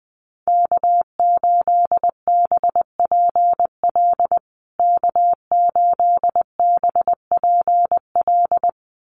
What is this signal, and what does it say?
Morse code, heard as a keyed tone, about 20 wpm, 700 Hz. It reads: K8BPL K8BPL
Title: USA Call Signs Repeated 2x 20wpm